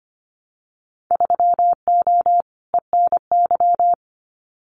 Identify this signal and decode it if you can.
Morse code, heard as a keyed tone, about 25 wpm, 700 Hz. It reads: 3O ENY
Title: Winter Field Day Exchanges Encore 25wpm